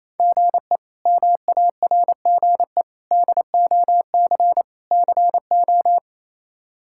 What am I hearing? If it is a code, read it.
Morse code, heard as a keyed tone, about 28 wpm, 700 Hz. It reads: GE MARGE DOC CO
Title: K1USN SST QSO Practice - 28wpm